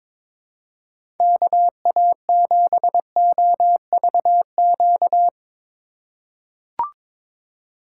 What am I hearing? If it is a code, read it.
Morse code, heard as a keyed tone, about 22 wpm, 700 Hz. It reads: KA7OVQ
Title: USA Call Signs - 6 Characters - 2x Character Spacing - 22wpm